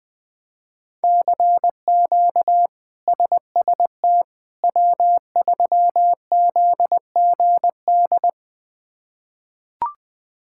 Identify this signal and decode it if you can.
Morse code, heard as a keyed tone, about 20 wpm, 700 Hz. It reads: CQ SST W3ZGD
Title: K1USN SST QSO Practice - 20wpm